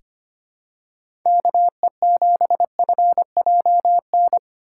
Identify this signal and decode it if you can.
Morse code, heard as a keyed tone, about 25 wpm, 700 Hz. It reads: KE7FJN
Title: USA Call Signs - 6x Character Spacing - 25wpm